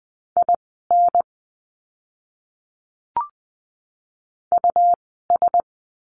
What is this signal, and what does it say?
Morse code, heard as a keyed tone, about 20 wpm, spaced as for 15 wpm, 700 Hz. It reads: IN US
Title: 146 Lesson 24f - Letter y - Words 20wpm